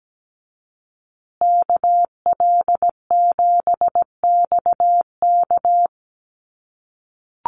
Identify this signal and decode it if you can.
Morse code, heard as a keyed tone, about 17 wpm, 700 Hz. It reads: KL7XK
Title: USA Call Signs Repeated 2x 17wpm